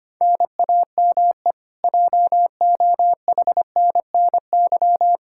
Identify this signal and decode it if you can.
Morse code, heard as a keyed tone, about 25 wpm, 700 Hz. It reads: NAME JOHNNY
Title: Practice QSO - Exchange 1 - Beginning - 25wpm